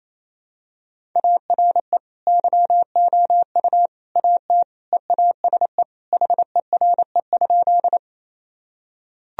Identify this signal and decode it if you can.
Morse code, heard as a keyed tone, about 28 wpm, 700 Hz. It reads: ARE YOU AT EASE HERE?
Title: Sentences from Top 1000 Words - Review Part 1 - 28wpm